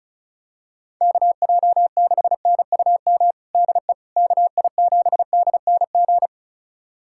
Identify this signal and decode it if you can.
Morse code, heard as a keyed tone, about 34 wpm, 700 Hz. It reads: KJ6NUM DE KI7DNG
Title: USA Callsign DE Callsign - 6 Characters - 35wpm